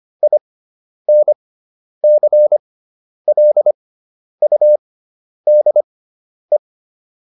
Morse code, heard as a keyed tone, about 25 wpm, spaced as for 10 wpm, 600 Hz. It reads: INCLUDE